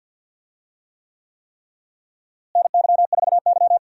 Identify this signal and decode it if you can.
Morse code, heard as a keyed tone, about 51 wpm, 700 Hz. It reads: NY4Y
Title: USA Call Signs - 4 Characters - Repeated 3x - 50wpm